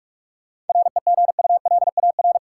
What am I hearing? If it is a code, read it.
Morse code, heard as a keyed tone, about 47 wpm, 700 Hz. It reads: REGULAR
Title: Sentences from Top 900 Words - New Word Drill - 45wpm